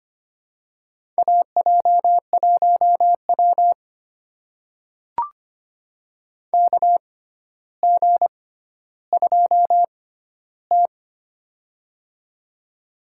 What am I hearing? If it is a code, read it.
Morse code, heard as a keyed tone, about 25 wpm, spaced as for 13 wpm, 700 Hz. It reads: AJ1W KG2T
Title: USA Call Signs - 4 Characters - 6x Character Spacing - 25wpm